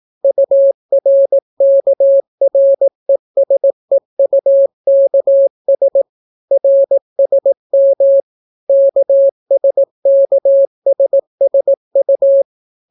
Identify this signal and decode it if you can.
Morse code, heard as a keyed tone, about 18 wpm, 550 Hz. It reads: URKRESEUKS RSM KSKSSU